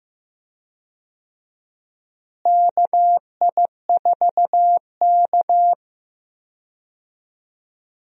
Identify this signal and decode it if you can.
Morse code, heard as a keyed tone, about 15 wpm, 700 Hz. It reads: KI4K
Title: USA Call Signs - 4 Characters - Repeated 3x - 15wpm